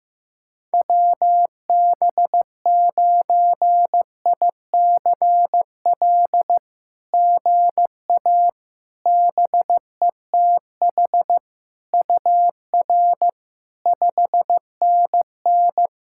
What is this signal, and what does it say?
Morse code, heard as a keyed tone, about 16 wpm, 700 Hz. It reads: WB9ICL GA BETH UR 5NN
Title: SOTA QSO Practice 15wpm